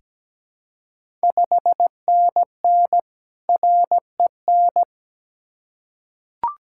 Morse code, heard as a keyed tone, about 17 wpm, 700 Hz. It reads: 5NN REN
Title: Ontario QSO Party Exchange 17wpm